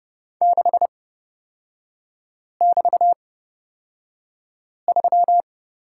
Morse code, heard as a keyed tone, about 29 wpm, spaced as for 5 wpm, 700 Hz. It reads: 6<BT>3
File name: Lesson 38e - Letter q - Three Characters 30wpm